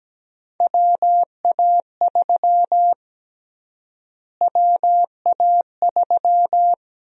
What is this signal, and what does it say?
Morse code, heard as a keyed tone, about 17 wpm, 700 Hz. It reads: WA3 WA3
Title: USA Call Sign Prefixes - Repeated 2x - 17wpm